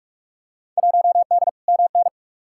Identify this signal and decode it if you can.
Morse code, heard as a keyed tone, about 45 wpm, 700 Hz. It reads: 1D MN
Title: Field Day Exchanges - 45wpm